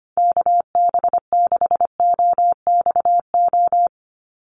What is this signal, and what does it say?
Morse code, heard as a keyed tone, about 25 wpm, 700 Hz. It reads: KB6OXO
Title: USA Call Signs 25wpm